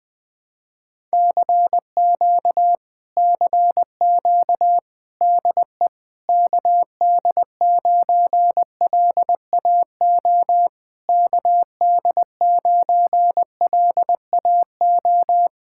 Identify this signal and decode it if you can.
Morse code, heard as a keyed tone, about 20 wpm, 700 Hz. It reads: CQ CQ DE KD9LAO KD9LAO
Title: Speed-Racing Common QSO Phrases Encore - 17wpm